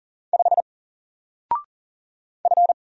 Morse code, heard as a keyed tone, about 40 wpm, 700 Hz. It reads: 5 F
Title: Single Letter-Number 40wpm